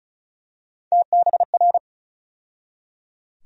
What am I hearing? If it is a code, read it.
Morse code, heard as a keyed tone, about 34 wpm, 700 Hz. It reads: TBR
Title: USA Call Sign Postfixes - 3 Letters 35wpm